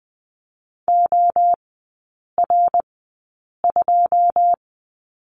Morse code, heard as a keyed tone, about 20 wpm, spaced as for 8 wpm, 700 Hz. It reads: OR2
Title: 100 Lesson 17e - Letter u - Three Characters 20wpm